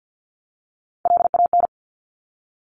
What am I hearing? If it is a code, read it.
Morse code, heard as a keyed tone, about 50 wpm, 700 Hz. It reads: LAN
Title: Top Three Letter Combinations in Words 50wpm